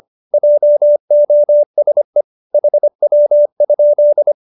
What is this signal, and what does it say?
Morse code, heard as a keyed tone, about 25 wpm, 570 Hz. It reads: JOSE HW?